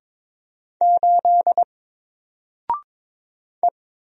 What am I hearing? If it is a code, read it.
Morse code, heard as a keyed tone, about 22 wpm, 700 Hz. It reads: 8 E
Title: Single Character (Letter:Number:Punctuation:Prosign) Alt 22wpm